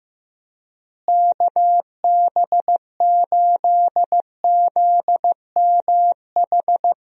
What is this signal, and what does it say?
Morse code, heard as a keyed tone, about 15 wpm, 700 Hz. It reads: KB8ZMH
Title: USA Call Signs - 6 Characters - Repeated 3x - 15wpm